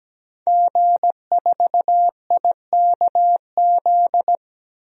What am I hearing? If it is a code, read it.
Morse code, heard as a keyed tone, about 17 wpm, 700 Hz. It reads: G4IKZ
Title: DXCC-100 Call Signs - 5 Characters - 17wpm